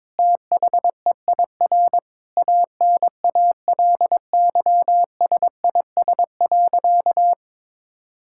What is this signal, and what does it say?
Morse code, heard as a keyed tone, about 22 wpm, 700 Hz. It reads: THEIR ANALYSIS.
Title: Sentences from Top 1000 Words - Review Part 5 - 22wpm